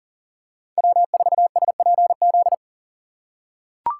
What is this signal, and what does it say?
Morse code, heard as a keyed tone, about 40 wpm, 700 Hz. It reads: W4SPZ
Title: USA Call Signs - 5 Characters - 2x Character Spacing - 40wpm